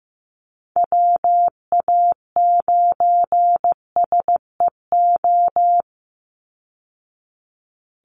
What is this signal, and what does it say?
Morse code, heard as a keyed tone, about 15 wpm, 700 Hz. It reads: WA9SEO
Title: Call Signs Repeated 2x - 15wpm